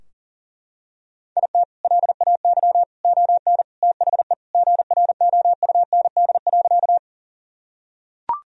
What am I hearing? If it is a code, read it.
Morse code, heard as a keyed tone, about 37 wpm, 700 Hz. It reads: IT LAY ON THE GROUND.
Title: Sentences from Top 1000 Words - Review Part 2 - 40wpm